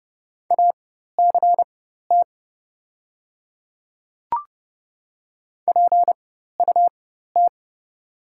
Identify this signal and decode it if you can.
Morse code, heard as a keyed tone, about 30 wpm, spaced as for 14 wpm, 700 Hz. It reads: ACT PUT
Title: 4x Character Spacing - Top 500 Words - 30wpm